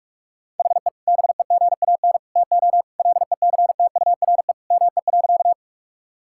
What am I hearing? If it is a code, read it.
Morse code, heard as a keyed tone, about 46 wpm, 700 Hz. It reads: HE BEGAN TO LECTURE ME.
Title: Sentences from Top 1600 Words 45wpm